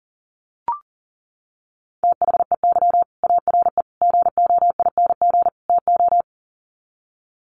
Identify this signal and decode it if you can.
Morse code, heard as a keyed tone, about 40 wpm, 700 Hz. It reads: THEY ARE GOING TO
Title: Sets of 4 Words 40wpm